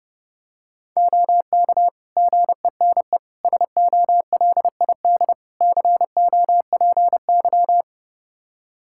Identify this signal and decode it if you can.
Morse code, heard as a keyed tone, about 30 wpm, 700 Hz. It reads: OK GENE SOLID COPY
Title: Practice QSO - Exchange 3 - More Info - 30wpm